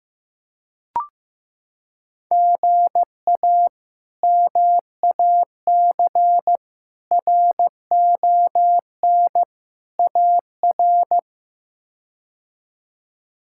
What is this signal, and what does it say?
Morse code, heard as a keyed tone, about 16 wpm, 700 Hz. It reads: GA MAC RON AR
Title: K1USN SST QSO Practice - 15wpm